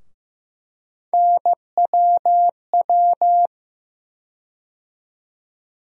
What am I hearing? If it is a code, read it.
Morse code, heard as a keyed tone, about 15 wpm, 700 Hz. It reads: NWW
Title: USA Call Sign Postfixes - 3 Letters 15wpm